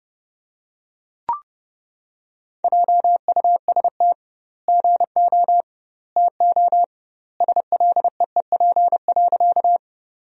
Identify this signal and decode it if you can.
Morse code, heard as a keyed tone, about 30 wpm, 700 Hz. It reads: JUST GO TO SLEEP.
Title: 2x Word-Spacing - Sentences from Top 500 Words - 30wpm